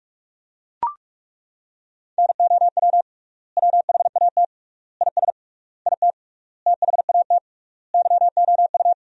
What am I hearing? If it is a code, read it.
Morse code, heard as a keyed tone, about 41 wpm, 700 Hz. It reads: NOW WHAT IS IT THAT YOU
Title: 3x Word-Spacing - Sentences from Top 200 Words - 45wpm